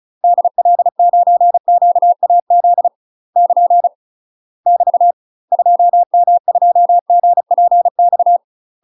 Morse code, heard as a keyed tone, about 36 wpm, 700 Hz. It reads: DL9QAZ <KN> <BT> 2M2GPX
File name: UK MASTER QSOs TRAINING / 35WPM / UK MASTER QSOs 0032